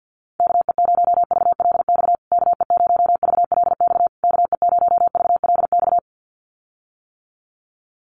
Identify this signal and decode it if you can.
Morse code, heard as a keyed tone, about 48 wpm, 700 Hz. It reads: KE0VLX KE0VLX KE0VLX
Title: USA Call Signs Repeated 3x 50wpm